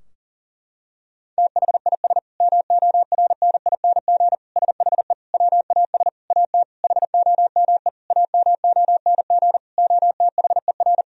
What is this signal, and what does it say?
Morse code, heard as a keyed tone, about 40 wpm, 700 Hz. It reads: THIS MORNING SHE WAS AT HOME AMONG OTHER
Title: Sentences from Top 1000 Words - Review Part 3 - 40wpm